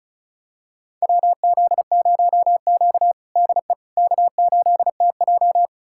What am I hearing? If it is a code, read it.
Morse code, heard as a keyed tone, about 36 wpm, 700 Hz. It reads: WZ0Q DE K8TJ
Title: USA Callsign DE Callsign - 4 Characters - 35wpm